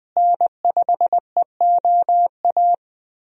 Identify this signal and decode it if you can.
Morse code, heard as a keyed tone, about 20 wpm, 700 Hz. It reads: N5EOA
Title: USA Call Signs - 5 Characters - Repeated 2x - 20wpm